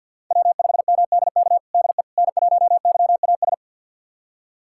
Morse code, heard as a keyed tone, about 52 wpm, 700 Hz. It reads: W5MDK DE N1YAS
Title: USA Callsign DE Callsign - 5 Characters - 50wpm